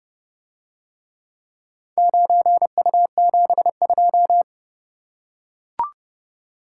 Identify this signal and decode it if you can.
Morse code, heard as a keyed tone, about 29 wpm, 700 Hz. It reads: 9U72